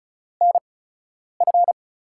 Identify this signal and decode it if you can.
Morse code, heard as a keyed tone, about 34 wpm, spaced as for 10 wpm, 700 Hz. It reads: NF
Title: USA Call Signs - 8x Character Spacing - 35wpm